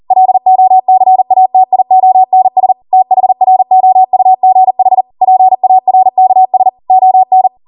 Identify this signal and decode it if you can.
Morse code, heard as a keyed tone, about 40 wpm, 750 Hz. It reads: LOCATIONS THROUGH PARKS ON